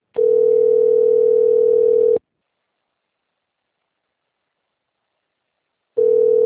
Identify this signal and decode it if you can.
Ringback tone (North America), 440 + 480 Hz.
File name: rg-1600-unknown- - - .12449